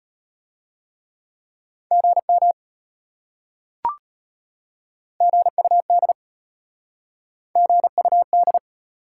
Morse code, heard as a keyed tone, about 33 wpm, 700 Hz. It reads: GM GUD GUD